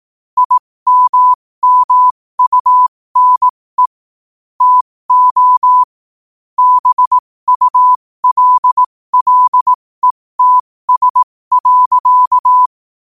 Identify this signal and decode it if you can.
Morse code, heard as a keyed tone, about 18 wpm, spaced as for 15 wpm, 1000 Hz. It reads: IMMUNE TO BULLETS.
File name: qotdfiles / 15wpm